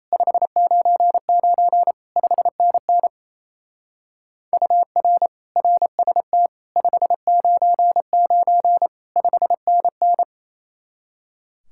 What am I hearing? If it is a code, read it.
Morse code, heard as a keyed tone, about 34 wpm, 700 Hz. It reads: 599 5NN UR RST 599 5NN
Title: Speed-Racing Common QSO Phrases - 28wpm